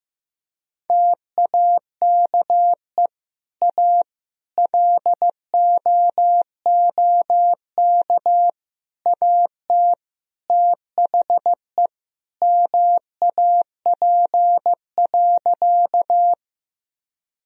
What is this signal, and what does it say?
Morse code, heard as a keyed tone, about 15 wpm, 700 Hz. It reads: TAKE A LOOK AT THE MAP.
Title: Sentences from Top 1000 Words - Review Part 2 - 15wpm